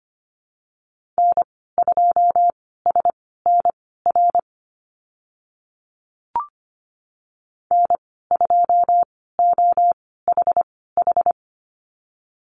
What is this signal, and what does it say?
Morse code, heard as a keyed tone, about 25 wpm, spaced as for 16 wpm, 690 Hz. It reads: N2SNR N2OHH
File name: Lesson 15g - Number 5 - Call Signs 25wpm